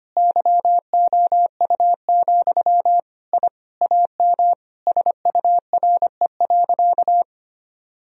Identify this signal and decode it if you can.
Morse code, heard as a keyed tone, about 25 wpm, 700 Hz. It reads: YOU, I AM SURE.
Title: Sentences from Top 1000 Words - Review Part 3 - 25wpm